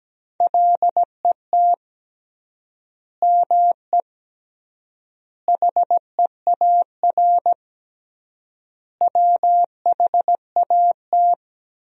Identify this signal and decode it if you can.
Morse code, heard as a keyed tone, about 17 wpm, 700 Hz. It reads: LET ME HEAR WHAT
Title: 3x Word-Spacing - Sentences from Top 300 Words - 17wpm